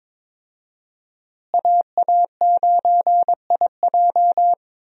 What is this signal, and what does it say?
Morse code, heard as a keyed tone, about 22 wpm, 700 Hz. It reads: AA9IJ
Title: USA Call Signs - 5 Characters - Repeated 2x - 22wpm